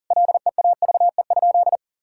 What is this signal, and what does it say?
Morse code, heard as a keyed tone, about 37 wpm, 700 Hz. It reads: LEAVE?
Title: Sentences from Top 1000 Words - Review Part 5 - 40wpm